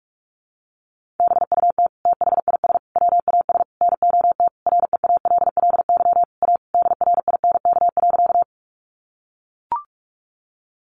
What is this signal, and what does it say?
Morse code, heard as a keyed tone, about 44 wpm, 700 Hz. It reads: BUT THIS WAS NOT REALLY A DRINK.
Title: Sentences from Top 1500 Words 45wpm